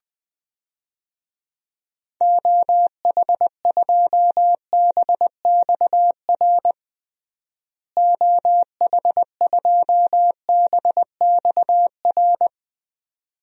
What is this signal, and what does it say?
Morse code, heard as a keyed tone, about 20 wpm, 700 Hz. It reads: OH2BXR OH2BXR
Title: DXCC-100 Call Signs - 6 Characters - Repeated 2x - 20wpm